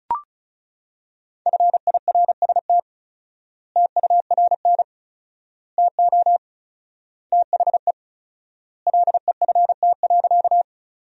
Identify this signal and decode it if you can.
Morse code, heard as a keyed tone, about 34 wpm, 700 Hz. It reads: FIRST TURN TO THE LEFT.
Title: 4x Word-Spacing - Sentences from Top 300 Words - 35wpm